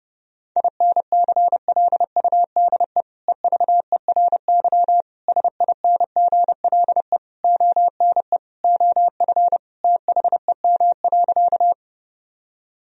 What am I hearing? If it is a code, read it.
Morse code, heard as a keyed tone, about 32 wpm, 700 Hz. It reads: INCLUDE EVERY SINGLE ONE OF THEM.
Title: Sentences from Top 1000 Words - Review Part 4 - 30wpm